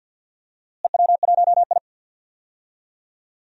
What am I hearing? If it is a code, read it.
Morse code, heard as a keyed tone, about 51 wpm, 700 Hz. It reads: EW1I
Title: DXCC-100 Call Signs - 4 Characters - Repeated 2x - 50wpm